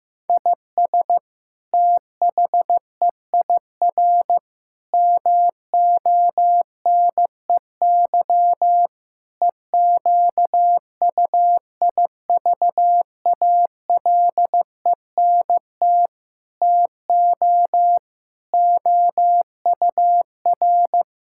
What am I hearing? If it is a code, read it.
Morse code, heard as a keyed tone, about 15 wpm, 700 Hz. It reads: IS THEIR MONEY EQUIVALENT TO OUR
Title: Sentences from Top 1600 Words 15wpm